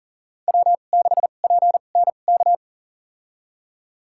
Morse code, heard as a keyed tone, about 40 wpm, spaced as for 29 wpm, 700 Hz. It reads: W6PNK
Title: USA Call Signs - 5 Characters - 2x Character Spacing - 40wpm